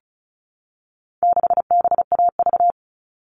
Morse code, heard as a keyed tone, about 34 wpm, 700 Hz. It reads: 6BAV
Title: All Cut Numbers - 4-Digit 35wpm